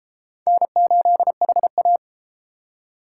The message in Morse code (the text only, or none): N8HA